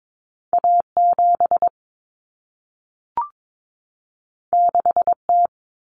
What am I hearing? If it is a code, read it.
Morse code, heard as a keyed tone, about 22 wpm, 700 Hz. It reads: A7 6T
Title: Common Cut Numbers - 2-Digit 22wpm